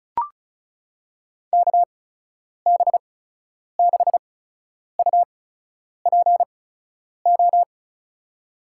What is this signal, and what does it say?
Morse code, heard as a keyed tone, about 36 wpm, spaced as for 10 wpm, 700 Hz. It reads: KB6UPO
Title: USA Call Signs - 6 Characters - 8x Character Spacing - 35wpm